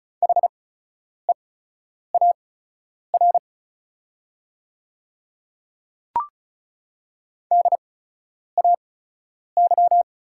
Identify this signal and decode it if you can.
Morse code, heard as a keyed tone, about 36 wpm, spaced as for 10 wpm, 700 Hz. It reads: HEAR DAY
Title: 8x Character Spacing - Top 300 Words - 35wpm